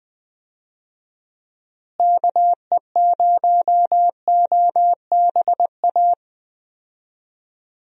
Morse code, heard as a keyed tone, about 20 wpm, 700 Hz. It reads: KE0OBA
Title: USA Call Signs - 6 Characters - Repeated 2x - 20wpm